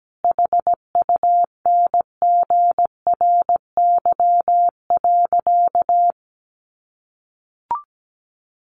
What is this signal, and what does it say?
Morse code, heard as a keyed tone, about 17 wpm, 700 Hz. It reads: HUNGRY.
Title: Sentences from Top 1400 Words 17wpm